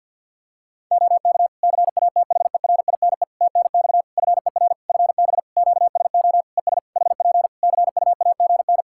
Morse code, heard as a keyed tone, about 46 wpm, 700 Hz. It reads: OK CATHERINE TNX FER FB QSO ES HP CUAGN
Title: Practice QSO - Exchange 4 - Ending - 50wpm